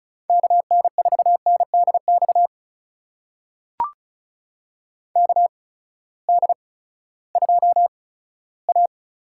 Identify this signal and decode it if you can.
Morse code, heard as a keyed tone, about 36 wpm, 700 Hz. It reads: K N 4 N D X K D 2 A
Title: USA Call Signs - 6 Characters - 8x Character Spacing - 35wpm